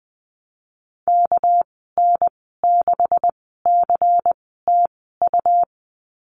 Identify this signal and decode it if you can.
Morse code, heard as a keyed tone, about 20 wpm, spaced as for 15 wpm, 700 Hz. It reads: KN6CTU
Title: 213 Lesson 34g - Number 8 - Call Signs 20wpm